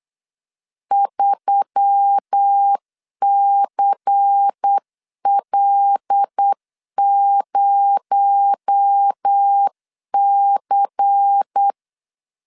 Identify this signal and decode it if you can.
Morse code, heard as a keyed tone, about 8 wpm, 800 Hz. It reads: 3CL0C
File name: CW / DET / 1 6 1500C S C 8wpm 5219